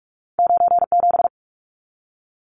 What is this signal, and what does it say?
Morse code, heard as a keyed tone, about 46 wpm, 700 Hz. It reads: 97
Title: Numbers - 2-Digit 45wpm